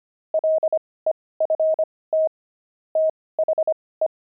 Morse code, heard as a keyed tone, about 25 wpm, spaced as for 18 wpm, 630 Hz. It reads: LEFT THE
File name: pangram-fs-18-cs-25-steaks